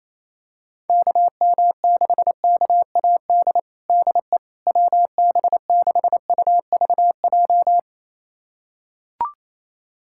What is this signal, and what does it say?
Morse code, heard as a keyed tone, about 28 wpm, 700 Hz. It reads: KM6KAD DE WB6UVJ
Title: USA Callsign DE Callsign - 6 Characters - 28wpm